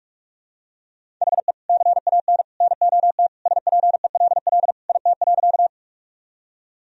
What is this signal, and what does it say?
Morse code, heard as a keyed tone, about 46 wpm, 700 Hz. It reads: HE CAN NOT SPELL IT.
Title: Sentences from Top 1000 Words - Review Part 1 - 45wpm